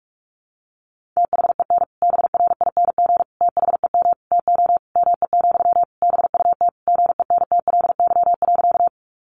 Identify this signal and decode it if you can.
Morse code, heard as a keyed tone, about 44 wpm, 700 Hz. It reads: THEN BRING THEM TO ME, BUT GENTLY.
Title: Sentences from Top 1500 Words 45wpm